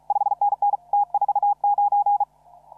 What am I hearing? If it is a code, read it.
Morse code, heard as a keyed tone, about 37 wpm, 800 Hz. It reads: 5NN T49
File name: CW / RHOB CW 2017 HA6P 0001